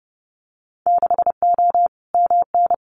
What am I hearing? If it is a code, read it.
Morse code, heard as a keyed tone, about 30 wpm, 700 Hz. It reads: 6O MN
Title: Winter Field Day Exchanges Encore 30wpm